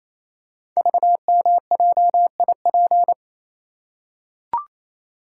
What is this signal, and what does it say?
Morse code, heard as a keyed tone, about 27 wpm, 700 Hz. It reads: VMJIP
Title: Code Groups - 5 Letter:Numbers - 28wpm